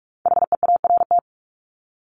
Morse code, heard as a keyed tone, about 44 wpm, 700 Hz. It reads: HEART
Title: Top 1001-1250 Words 45wpm